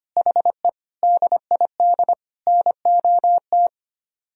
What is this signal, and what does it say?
Morse code, heard as a keyed tone, about 31 wpm, 700 Hz. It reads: HE DID NOT
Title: Speed-Racing - Sets of 4 Words 25wpm